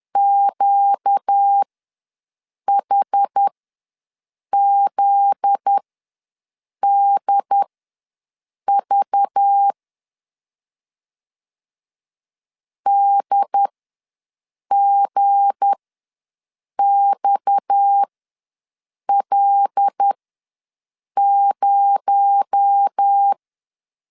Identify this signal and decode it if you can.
Morse code, heard as a keyed tone, about 10 wpm, spaced as for 6 wpm, 800 Hz. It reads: QHZDV DGXL0
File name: LEZIONI / 8 YJ45 / 8 Y ws 10wpm 3952